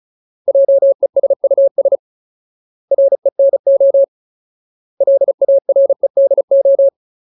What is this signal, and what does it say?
Morse code, heard as a keyed tone, about 35 wpm, 550 Hz. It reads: JESUS RENO LAREDO
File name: Ultimate Word List / 35WPM Ultimate